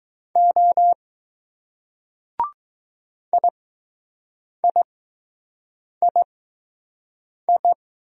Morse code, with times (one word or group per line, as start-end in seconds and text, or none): 0.4-0.9 O
3.3-7.7 IIII